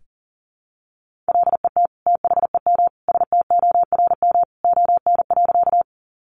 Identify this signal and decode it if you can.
Morse code, heard as a keyed tone, about 37 wpm, 700 Hz. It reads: LET THEM STORM ON.
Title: Sentences from Top 1500 Words 40wpm